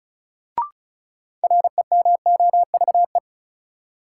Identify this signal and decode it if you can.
Morse code, heard as a keyed tone, about 34 wpm, 700 Hz. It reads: REMOVE
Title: Sentences from Top 800 Words - New Word Drill - 35wpm